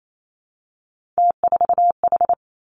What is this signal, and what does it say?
Morse code, heard as a keyed tone, about 28 wpm, 700 Hz. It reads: T4H
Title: Three Challenging Letter-Number Combinations 28wpm